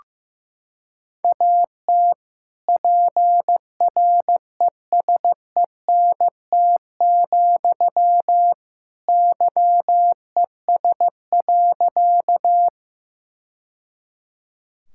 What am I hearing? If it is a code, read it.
Morse code, heard as a keyed tone, about 15 wpm, 700 Hz. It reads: AT PRESENT, YES.
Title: Sentences from Top 1000 Words - Review Part 5 - 15wpm